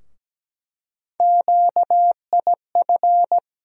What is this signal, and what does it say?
Morse code, heard as a keyed tone, about 17 wpm, 700 Hz. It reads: QIF
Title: USA Call Sign Postfixes - 3 Letters 17wpm